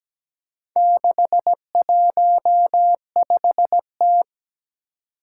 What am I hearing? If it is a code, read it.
Morse code, heard as a keyed tone, about 17 wpm, 700 Hz. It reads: 615T
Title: Four Challenging Letter-Number Combinations 17wpm